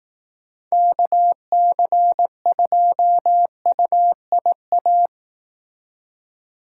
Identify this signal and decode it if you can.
Morse code, heard as a keyed tone, about 18 wpm, 700 Hz. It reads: KC2UIA
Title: Speed Racing - USA 2x3 Call Signs 15wpm